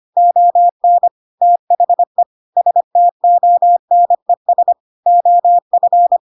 Morse code, heard as a keyed tone, about 25 wpm, 700 Hz. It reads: ON THE STONES OF
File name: Pinocchio / Pinocchio ch3 25 25 ik1hge